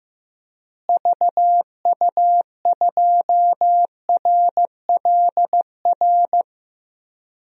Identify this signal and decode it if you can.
Morse code, heard as a keyed tone, about 14 wpm, 700 Hz. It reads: VU2RLR
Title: DXCC-100 Call Signs - 6 Characters - Repeated 3x - 15wpm